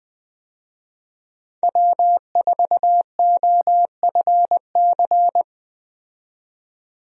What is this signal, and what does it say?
Morse code, heard as a keyed tone, about 20 wpm, 700 Hz. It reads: W4OFC
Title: USA Call Signs - 5 Characters - 20wpm